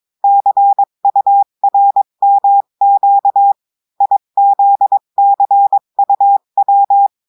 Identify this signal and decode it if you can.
Morse code, heard as a keyed tone, about 22 wpm, 800 Hz. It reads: CURMQ IZCUW
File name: cours-du-mercredi / audios / cours-853-du-2-mai-2024-etxte chiffre et texte clair-a-20-wpm